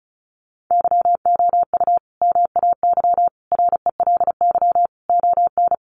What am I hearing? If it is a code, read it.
Morse code, heard as a keyed tone, about 34 wpm, 700 Hz. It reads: YOU MAY RELY ON
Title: Sentences from Top 1200 Words 35wpm